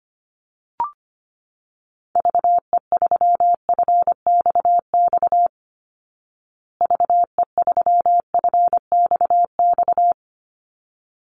Morse code, heard as a keyed tone, about 25 wpm, 700 Hz. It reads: VE3FXX VE3FXX
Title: Call Signs Repeated 3x - 25wpm